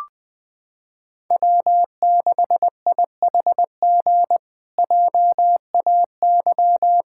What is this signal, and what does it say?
Morse code, heard as a keyed tone, about 20 wpm, 700 Hz. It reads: W6IHG JAY
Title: K1USN SST QSO Practice - 20wpm